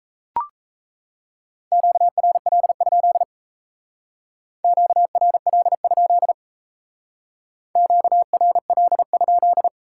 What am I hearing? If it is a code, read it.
Morse code, heard as a keyed tone, about 36 wpm, 700 Hz. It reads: QRL? QRL? QRL?